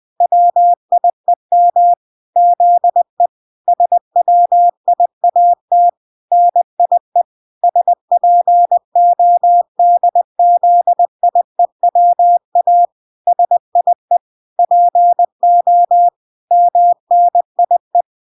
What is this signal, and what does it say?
Morse code, heard as a keyed tone, about 20 wpm, 700 Hz. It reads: WIEM ZE SWIAT NIE SPODZIEWA SIE PO MNIE